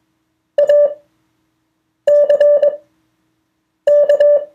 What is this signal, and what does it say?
Morse code, heard as a keyed tone, about 21 wpm, spaced as for 6 wpm, 580 Hz. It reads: ACK
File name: 2of4-words-1